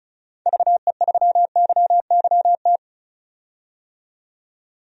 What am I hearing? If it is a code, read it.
Morse code, heard as a keyed tone, about 36 wpm, 700 Hz. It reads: VE3YYT
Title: DXCC-100 Call Signs - 6 Characters - 35wpm